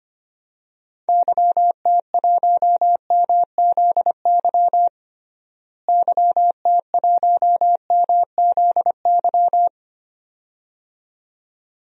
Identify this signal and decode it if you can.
Morse code, heard as a keyed tone, about 25 wpm, 700 Hz. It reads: YT1MZY YT1MZY
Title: DXCC-100 Call Signs - 6 Characters - Repeated 2x - 25wpm